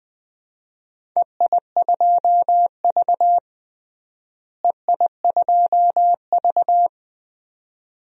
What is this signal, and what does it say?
Morse code, heard as a keyed tone, about 20 wpm, 700 Hz. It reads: EI2V EI2V
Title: DXCC-100 Call Signs - 4 Characters - Repeated 2x - 20wpm